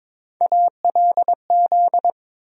Morse code, heard as a keyed tone, about 21 wpm, 700 Hz. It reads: ALZ
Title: USA Call Sign Postfixes - 3 Letters 22wpm